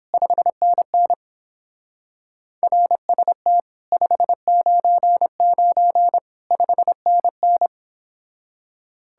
Morse code, heard as a keyed tone, about 26 wpm, 700 Hz. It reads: 5NN RST 599 5NN
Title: Speed-Racing Common QSO Phrases Encore - 22wpm